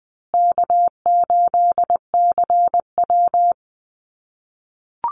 Morse code, heard as a keyed tone, about 20 wpm, 700 Hz. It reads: K8CW
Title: USA Call Signs Repeated 3x 20wpm